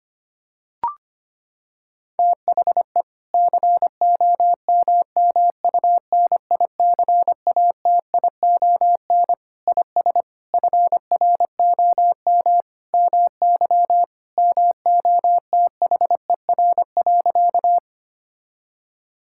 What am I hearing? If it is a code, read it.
Morse code, heard as a keyed tone, about 25 wpm, 700 Hz. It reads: THE COMMUNICATION IS FROM MY MOTHER.
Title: Sentences from Top 1000 Words - Review Part 6 - 25wpm